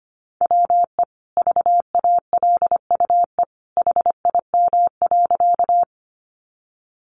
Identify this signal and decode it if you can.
Morse code, heard as a keyed tone, about 25 wpm, 700 Hz. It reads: WE VALUE HIM.
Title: Sentences from Top 600 Words 25wpm